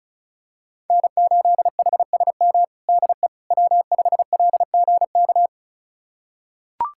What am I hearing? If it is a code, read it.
Morse code, heard as a keyed tone, about 36 wpm, 700 Hz. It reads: N8HSM DE W5LGK
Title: USA Callsign DE Callsign - 5 Characters - 35wpm